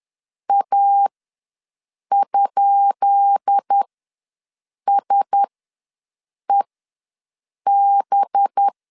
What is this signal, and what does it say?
Morse code, heard as a keyed tone, about 10 wpm, spaced as for 6 wpm, 800 Hz. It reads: A?SEB
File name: LEZIONI / 10 SIGNS / 10B INT ws 10wpm 4134